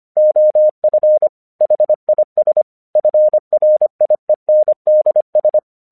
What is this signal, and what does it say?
Morse code, heard as a keyed tone, about 25 wpm, 610 Hz. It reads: OF HIS FRIENDS
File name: 1 English 25wpm / AQC17